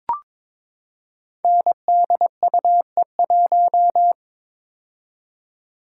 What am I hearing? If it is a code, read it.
Morse code, heard as a keyed tone, about 22 wpm, 700 Hz. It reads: NDUE1
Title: Code Groups - 5 Letter:Numbers - 22wpm